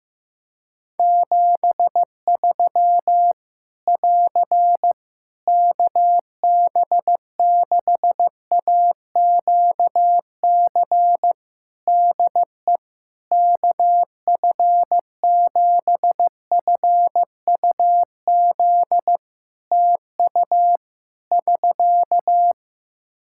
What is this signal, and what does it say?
Morse code, heard as a keyed tone, about 15 wpm, 700 Hz. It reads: 73 <AR> KB6AQC DE KF7FUZ TU <SK>
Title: Practice QSO - Exchange 4 - Ending - 15wpm